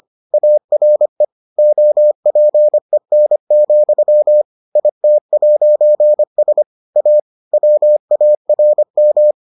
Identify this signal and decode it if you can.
Morse code, heard as a keyed tone, about 25 wpm, 600 Hz. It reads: ARE OPEN, IT'S A WARM